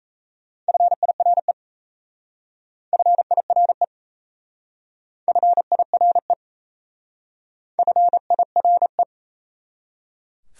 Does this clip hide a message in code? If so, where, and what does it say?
Yes: Morse code from 0.7 to 9.0 s, heard as a keyed tone, about 36 wpm, 700 Hz. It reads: FIRE FIRE FIRE FIRE